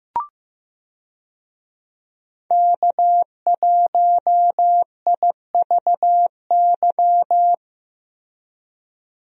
Morse code, heard as a keyed tone, about 15 wpm, 700 Hz. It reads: K1IVY
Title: USA Call Signs - 5 Characters - Repeated 2x - 15wpm